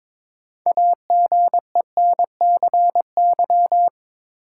Morse code, heard as a keyed tone, about 23 wpm, 700 Hz. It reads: AGENCY
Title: Sentences from Top 1300 Words - New Word Drill - 22wpm